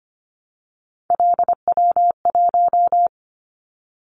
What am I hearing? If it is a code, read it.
Morse code, heard as a keyed tone, about 25 wpm, 700 Hz. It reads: LW1
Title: Callsign Prefixes for Top 100 DXCC - 25wpm